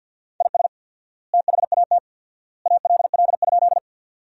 Morse code, heard as a keyed tone, about 46 wpm, 700 Hz. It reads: IS THAT ALL?
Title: 4x Word-Spacing - Sentences from Top 100 Words - 50wpm